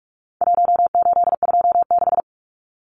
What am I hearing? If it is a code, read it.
Morse code, heard as a keyed tone, about 44 wpm, 700 Hz. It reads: 1826
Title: Numbers - Contest Style - 4-Digit 45wpm